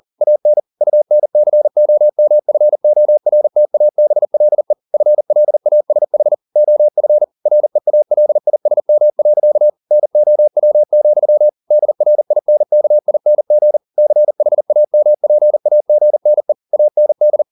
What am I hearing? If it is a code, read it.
Morse code, heard as a keyed tone, about 40 wpm, 600 Hz. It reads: AN UNCOMFORTABLE FLASH OF REALISM. NOW, DRINKING CHAMPAGNE AND